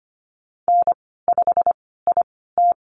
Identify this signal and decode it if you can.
Morse code, heard as a keyed tone, about 25 wpm, spaced as for 16 wpm, 700 Hz. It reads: N5IT
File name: Lesson 15g - Number 5 - Call Signs 25wpm